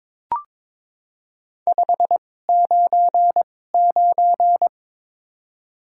Morse code, heard as a keyed tone, about 22 wpm, spaced as for 16 wpm, 700 Hz. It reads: 599
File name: Top 90 QSO Elements - 2x Character Spacing - 22wpm